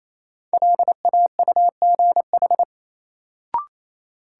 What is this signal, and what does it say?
Morse code, heard as a keyed tone, about 28 wpm, 700 Hz. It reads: LAUGH
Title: Sentences from Top 500 Words - New Word Drill - 28wpm